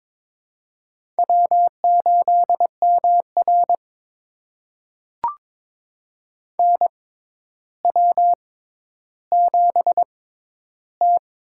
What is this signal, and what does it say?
Morse code, heard as a keyed tone, about 22 wpm, spaced as for 11 wpm, 700 Hz. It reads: W8MR NW7T
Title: USA Call Signs - 4 Characters - 6x Character Spacing - 22wpm